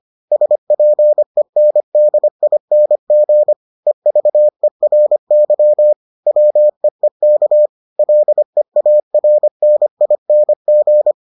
Morse code, heard as a keyed tone, about 25 wpm, 600 Hz. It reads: SPENDING EVERY WEEK LEARNING